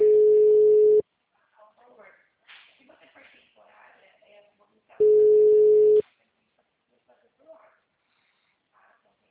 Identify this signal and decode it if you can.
Ringback tone (Europe, 425 Hz), with speech.
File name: out- -2006- - - .8865